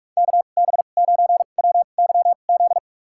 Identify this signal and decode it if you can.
Morse code, heard as a keyed tone, about 46 wpm, spaced as for 32 wpm, 700 Hz. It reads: KB9WYZ
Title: USA Call Signs - 2x Character Spacing - 45wpm